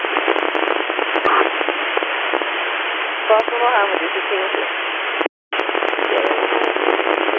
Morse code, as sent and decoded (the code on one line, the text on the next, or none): none